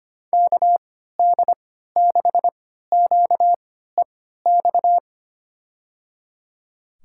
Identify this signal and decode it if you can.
Morse code, heard as a keyed tone, about 25 wpm, spaced as for 14 wpm, 700 Hz. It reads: KD6QEX
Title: USA Call Signs - 3x Character Spacing - 25wpm